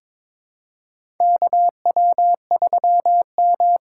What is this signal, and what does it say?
Morse code, heard as a keyed tone, about 21 wpm, 700 Hz. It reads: KW3M
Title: USA Call Signs - 4 Characters - 2x Character Spacing - 22wpm